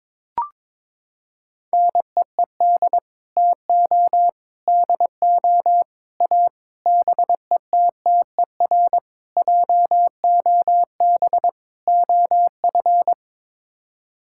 Morse code, heard as a keyed tone, about 22 wpm, 700 Hz. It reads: NEED TO DO A BETTER JOB OF
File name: practice / 22wpm / Sets of 7 Words - 22wpm